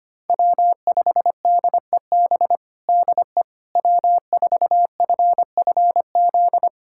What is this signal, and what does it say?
Morse code, heard as a keyed tone, about 25 wpm, 700 Hz. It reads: W5DEB DE W4FFZ
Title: USA Callsign DE Callsign - 5 Characters - 25wpm